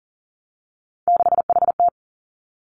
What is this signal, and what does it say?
Morse code, heard as a keyed tone, about 40 wpm, 700 Hz. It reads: 6HT
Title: Three Challenging Letter-Number Combinations 40wpm